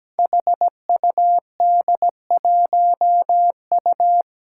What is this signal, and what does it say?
Morse code, heard as a keyed tone, about 17 wpm, 700 Hz. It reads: HUD1U
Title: Five Challenging Letter-Number Combinations 17wpm